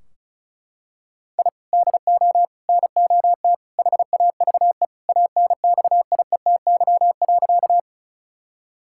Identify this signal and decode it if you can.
Morse code, heard as a keyed tone, about 36 wpm, 700 Hz. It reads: I DO NOT HAVE ANXIETY.
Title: Sentences from Top 1600 Words 35wpm